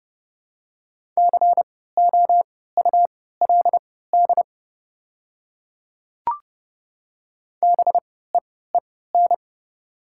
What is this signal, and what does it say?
Morse code, heard as a keyed tone, about 30 wpm, spaced as for 17 wpm, 700 Hz. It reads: COULD BEEN
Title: 3x Character Spacing - Top 100 Words - 30wpm